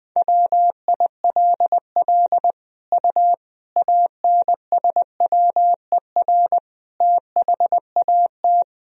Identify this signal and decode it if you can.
Morse code, heard as a keyed tone, about 20 wpm, 700 Hz. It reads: WILL U ANSWER THAT
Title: Sentences from Top 400 Words and Abbreviations 20wpm